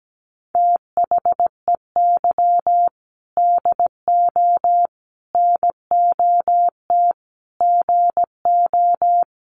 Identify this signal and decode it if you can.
Morse code, heard as a keyed tone, about 17 wpm, 700 Hz. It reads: THEY DO NOT GO
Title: Sentences from Top 700 Words 17wpm